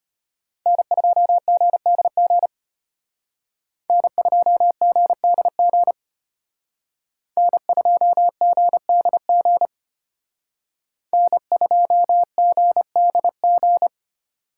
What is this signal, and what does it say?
Morse code, heard as a keyed tone, about 32 wpm, 700 Hz. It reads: N2GDG N2GDG N2GDG N2GDG